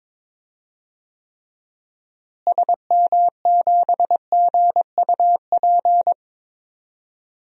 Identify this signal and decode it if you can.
Morse code, heard as a keyed tone, about 22 wpm, 700 Hz. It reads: SM7GUP
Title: DXCC-100 Call Signs - 6 Characters - Repeated 2x - 22wpm